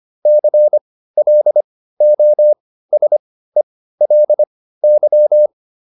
Morse code, heard as a keyed tone, about 25 wpm, spaced as for 15 wpm, 600 Hz. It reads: CLOSELY